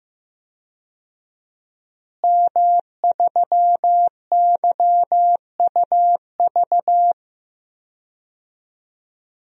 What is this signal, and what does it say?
Morse code, heard as a keyed tone, about 15 wpm, 700 Hz. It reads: M3YUV
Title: DXCC-100 Call Signs - 5 Characters - Repeated 2x - 15wpm